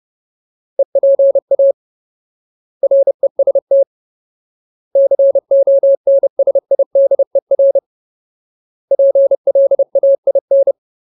Morse code, heard as a keyed tone, about 30 wpm, 550 Hz. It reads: EPA REST CONSIDER PLAIN
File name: Ultimate Word List / 30WPM Ultimate